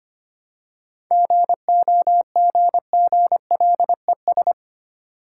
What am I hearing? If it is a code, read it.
Morse code, heard as a keyed tone, about 25 wpm, 690 Hz. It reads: GOGGLES
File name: practice / 25wpm / Easy Words with Hints - 25wpm